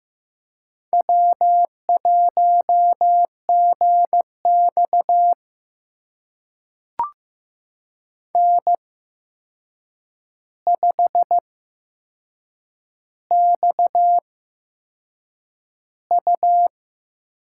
Morse code, heard as a keyed tone, about 15 wpm, spaced as for 6 wpm, 700 Hz. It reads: W1GX N5XU
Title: USA Call Signs - 4 Characters - 8x Character Spacing - 15wpm